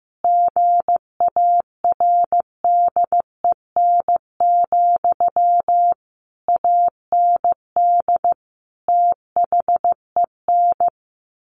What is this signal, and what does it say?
Morse code, heard as a keyed tone, about 15 wpm, 700 Hz. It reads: GARDEN, AND THEN
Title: Sentences from Top 700 Words 15wpm